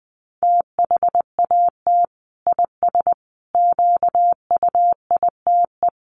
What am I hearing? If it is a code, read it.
Morse code, heard as a keyed tone, about 20 wpm, 700 Hz. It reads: THAT IS QUITE
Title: Sentences from Top 1100 Words 20wpm